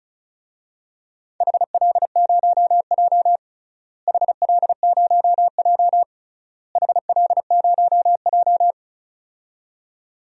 Morse code, heard as a keyed tone, about 34 wpm, 700 Hz. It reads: HL0J HL0J HL0J
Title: DXCC-100 Call Signs - 4 Characters - Repeated 3x - 35wpm